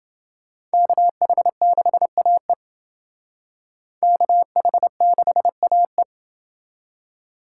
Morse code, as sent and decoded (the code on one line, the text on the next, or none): -.- .... -.... .- . / -.- .... -.... .- .
KH6AE KH6AE